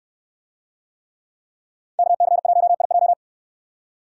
Morse code, heard as a keyed tone, about 69 wpm, 690 Hz. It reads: KC1IJ